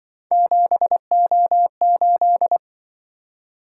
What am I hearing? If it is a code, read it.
Morse code, heard as a keyed tone, about 23 wpm, 700 Hz. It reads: 7O8
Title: Speed-Racing - Three Letter-Number Combinations - 20wpm